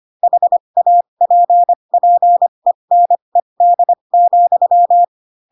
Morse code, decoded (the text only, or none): HAPPENED,